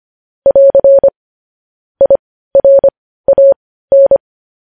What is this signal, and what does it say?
Morse code, heard as a keyed tone, about 25 wpm, spaced as for 15 wpm, 550 Hz. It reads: <AR> IRAN